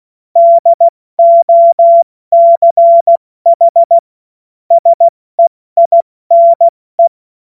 Morse code, heard as a keyed tone, about 16 wpm, 680 Hz. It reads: DOCH SEINE
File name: Morsekurs / Witze / Witz 25. Oktober 2022 70 F80